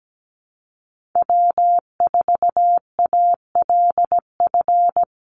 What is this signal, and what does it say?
Morse code, heard as a keyed tone, about 17 wpm, 700 Hz. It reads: W4ALF
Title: Call Signs - 17wpm